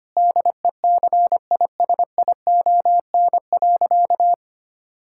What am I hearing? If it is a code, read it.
Morse code, heard as a keyed tone, about 25 wpm, 700 Hz. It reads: DECISION.
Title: Sentences from Top 1600 Words 25wpm